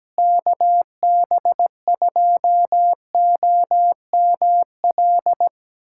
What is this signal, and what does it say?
Morse code, heard as a keyed tone, about 17 wpm, 700 Hz. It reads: KB2OML
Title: USA Callsign DE Callsign - 6 Characters - 17wpm